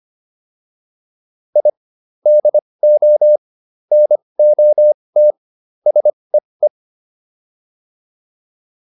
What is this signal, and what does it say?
Morse code, heard as a keyed tone, about 25 wpm, spaced as for 20 wpm, 600 Hz. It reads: I DO NOT SEE